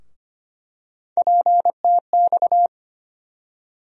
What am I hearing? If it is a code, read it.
Morse code, heard as a keyed tone, about 25 wpm, 700 Hz. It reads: PTX
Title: USA Call Sign Postfixes - 3 Letters 25wpm